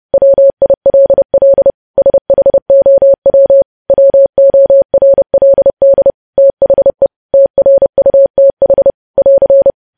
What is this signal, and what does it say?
Morse code, heard as a keyed tone, about 30 wpm, 550 Hz. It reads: WILL SHOW WORLD THE TRUTH <AR>
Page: news headlines in morse code at 30 wpm